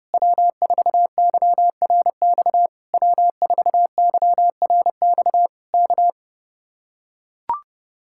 Morse code, heard as a keyed tone, about 30 wpm, 700 Hz. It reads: W4YRX W4YRX K